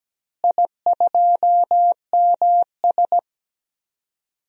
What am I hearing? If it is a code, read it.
Morse code, heard as a keyed tone, about 17 wpm, 700 Hz. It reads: I2MS
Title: DXCC-100 Call Signs - 4 Characters - Repeated 2x - 17wpm